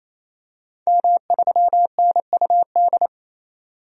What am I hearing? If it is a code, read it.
Morse code, heard as a keyed tone, about 28 wpm, 700 Hz. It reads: M3NUD
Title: DXCC-100 Call Signs - 5 Characters - 28wpm